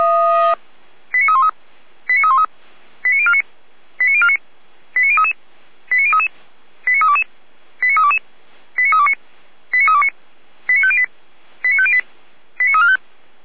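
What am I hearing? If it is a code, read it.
Five-tone selective call (ZVEI), heard as quick runs of five tones, tones of about 70 ms. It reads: 89313 89313 89049 89049 89033 89033 89322 89322 89329 89329 89508 89508 89345
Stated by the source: ZVEI five-tone selective call received on an SDR, from the Feuerwehr Orlamünde audio archive.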